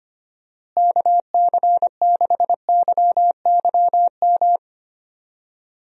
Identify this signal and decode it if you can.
Morse code, heard as a keyed tone, about 25 wpm, 700 Hz. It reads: KC6YYM
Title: USA Call Signs - 6 Characters - Repeated 2x - 25wpm